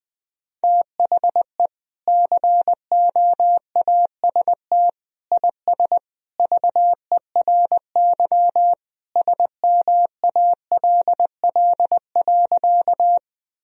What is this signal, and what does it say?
Morse code, heard as a keyed tone, about 21 wpm, 700 Hz. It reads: THE COAST IS VERY SMALL.
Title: Sentences from Top 1000 Words - Review Part 4 - 20wpm